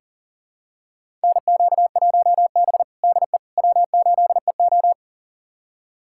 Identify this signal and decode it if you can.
Morse code, heard as a keyed tone, about 40 wpm, 700 Hz. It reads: NQ1B DE W8EO
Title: USA Callsign DE Callsign - 4 Characters - 40wpm